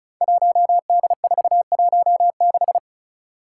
Morse code, heard as a keyed tone, about 36 wpm, 700 Hz. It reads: 1D416
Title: Five Challenging Letter-Number Combinations 35wpm